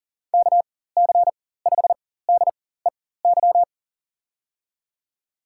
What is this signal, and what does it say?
Morse code, heard as a keyed tone, about 39 wpm, spaced as for 19 wpm, 700 Hz. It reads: KC5DEY
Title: USA Call Signs - 6 Characters - 4x Character Spacing - 40wpm